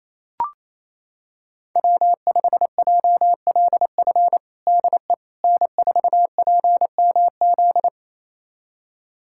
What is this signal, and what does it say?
Morse code, heard as a keyed tone, about 28 wpm, 700 Hz. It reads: W5JLF DE N4PMZ
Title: USA Callsign DE Callsign - 5 Characters - 28wpm